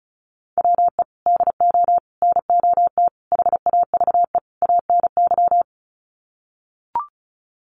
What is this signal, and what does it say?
Morse code, heard as a keyed tone, about 34 wpm, 700 Hz. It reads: WE DO NOT HAVE ANY
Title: Sets of 5 Words 35wpm